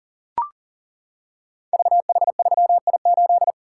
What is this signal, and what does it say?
Morse code, heard as a keyed tone, about 38 wpm, 700 Hz. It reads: VH3I8
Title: Code Groups - 5 Letter:Numbers - 40wpm